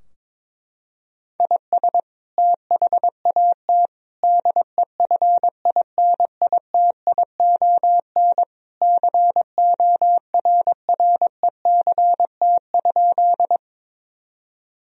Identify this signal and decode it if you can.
Morse code, heard as a keyed tone, about 22 wpm, 700 Hz. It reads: IS THAT DEFINITION CORRECT?
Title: Sentences from Top 1000 Words - Review Part 4 - 22wpm